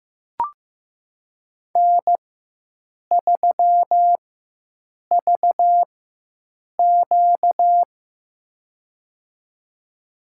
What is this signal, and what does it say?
Morse code, heard as a keyed tone, about 15 wpm, spaced as for 7 wpm, 700 Hz. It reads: N3VQ
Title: USA Call Signs - 4 Characters - 4x Character Spacing - 15wpm